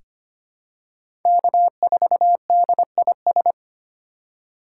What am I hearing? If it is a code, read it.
Morse code, heard as a keyed tone, about 25 wpm, 700 Hz. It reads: K4DIS
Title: USA Call Signs - 5 Characters - 2x Character Spacing - 25wpm